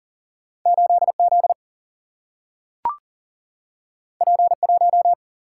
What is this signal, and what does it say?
Morse code, heard as a keyed tone, about 39 wpm, 700 Hz. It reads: 8 Z P 1
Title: Single-Letter Single-Number 40wpm